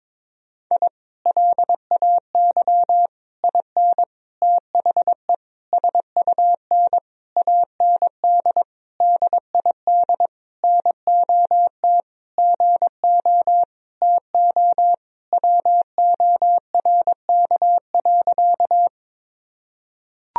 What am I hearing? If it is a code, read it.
Morse code, heard as a keyed tone, about 22 wpm, 700 Hz. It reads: I LAY IN THE SUN AND DID NOT GO TO WORK.
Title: Sentences from Top 1000 Words - Review Part 2 - 22wpm